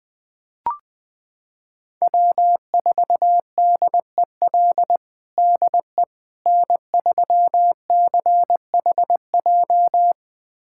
Morse code, heard as a keyed tone, about 21 wpm, 700 Hz. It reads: W4DEL DE N3CHJ
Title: USA Callsign DE Callsign - 5 Characters - 20wpm